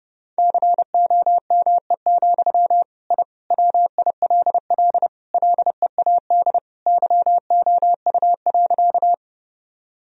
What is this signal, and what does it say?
Morse code, heard as a keyed tone, about 30 wpm, 700 Hz. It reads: COME, I WILL LEAD YOU.
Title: Sentences from Top 1000 Words - Review Part 2 - 30wpm